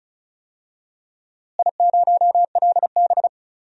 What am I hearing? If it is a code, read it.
Morse code, heard as a keyed tone, about 34 wpm, 700 Hz. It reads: I0LB
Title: DXCC-100 Call Signs - 4 Characters - Repeated 3x - 35wpm